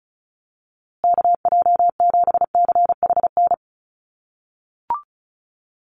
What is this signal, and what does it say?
Morse code, heard as a keyed tone, about 34 wpm, 700 Hz. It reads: KJ7CHN
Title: USA Call Signs Repeated 3x 35wpm